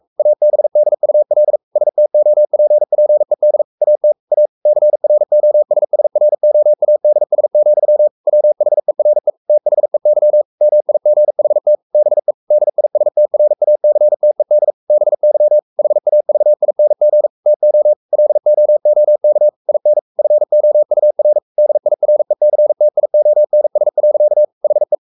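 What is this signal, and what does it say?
Morse code, heard as a keyed tone, about 44 wpm, 600 Hz. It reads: ABDUL STOPPED AT A CROSSROADS, WHERE THEY MIGHT BE DISTRACTED BY HAVING TO LOOK IN FOUR DIRECTIONS. HE
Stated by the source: Morse code at 43 wpm sending Ken Follett's novel Never in English, from CW med Gnister.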